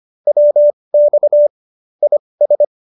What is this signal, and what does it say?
Morse code, heard as a keyed tone, about 25 wpm, spaced as for 20 wpm, 590 Hz. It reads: WX IS